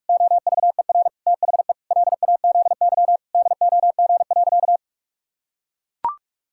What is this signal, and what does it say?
Morse code, heard as a keyed tone, about 42 wpm, 700 Hz. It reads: OVER THE LAZY DOG.
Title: Pangram Challenge v2 - 45wpm